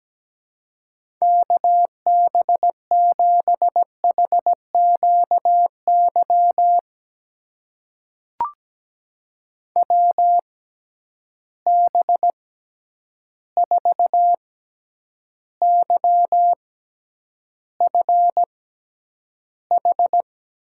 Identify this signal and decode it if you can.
Morse code, heard as a keyed tone, about 17 wpm, spaced as for 9 wpm, 700 Hz. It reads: KB7HQY WB4YFH
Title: USA Call Signs - 6 Characters - 6x Character Spacing - 17wpm